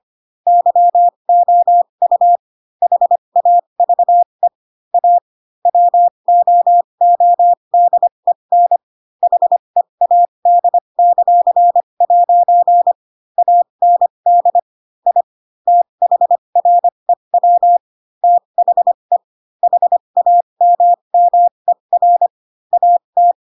Morse code, heard as a keyed tone, about 25 wpm, spaced as for 22 wpm, 700 Hz. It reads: YOU HAVE A WOODEN HEAD;' AND I THREW THE HAMMER AT
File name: Pinocchio / Pinocchio ch7 18 25 ik1hge